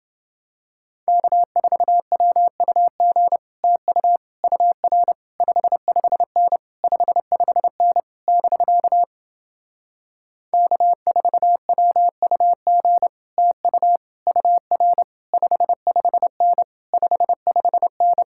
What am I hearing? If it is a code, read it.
Morse code, heard as a keyed tone, about 30 wpm, 700 Hz. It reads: K4WUG TU UR 55N 55N <BK> K4WUG TU UR 55N 55N